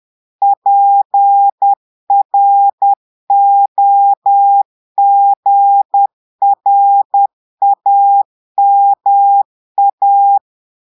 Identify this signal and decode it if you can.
Morse code, heard as a keyed tone, about 10 wpm, 800 Hz. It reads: PROGRAMA
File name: portugais texte en clair 2 vitesse 10 mots minutes